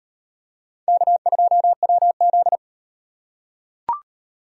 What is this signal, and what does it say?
Morse code, heard as a keyed tone, about 38 wpm, 700 Hz. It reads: K2WZ